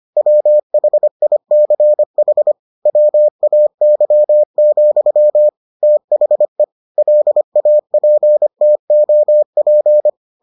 Morse code, heard as a keyed tone, about 25 wpm, 600 Hz. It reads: WHICH WAY, THE LAPTOP